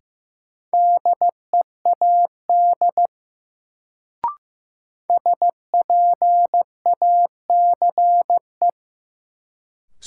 Morse code, heard as a keyed tone, about 15 wpm, 700 Hz. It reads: DEAD SPACE
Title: Sentences from Top 1500 Words - New Word Drill Alt - 15wpm